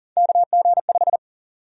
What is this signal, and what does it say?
Morse code, heard as a keyed tone, about 40 wpm, 700 Hz. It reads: KG5
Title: USA Call Sign Prefixes - 40wpm